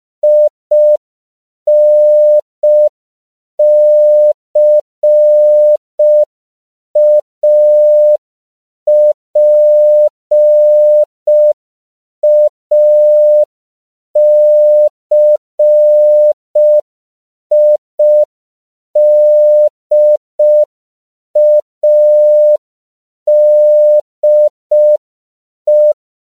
Morse code, heard as a keyed tone, about 5 wpm, 600 Hz. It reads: INCAPACIDADE
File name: CW / 5PPM